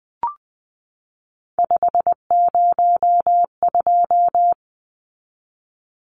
Morse code, heard as a keyed tone, about 20 wpm, 700 Hz. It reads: 502
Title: Numbers - 3-Digit 20wpm